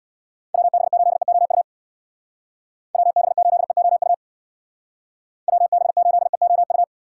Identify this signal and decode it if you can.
Morse code, heard as a keyed tone, about 64 wpm, 700 Hz. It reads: WB8EOU WB8EOU WB8EOU